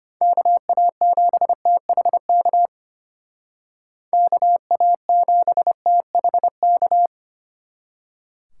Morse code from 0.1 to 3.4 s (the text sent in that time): KA7THK